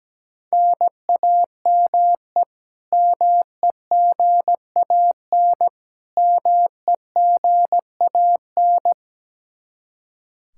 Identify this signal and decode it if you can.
Morse code, heard as a keyed tone, about 17 wpm, 700 Hz. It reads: NAME MEGAN MEGAN
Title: Practice QSO - Exchange 1 - Beginning - 17wpm